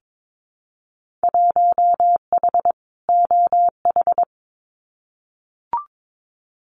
Morse code, heard as a keyed tone, about 22 wpm, 700 Hz. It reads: 1H OH
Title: Winter Field Day Exchanges 22wpm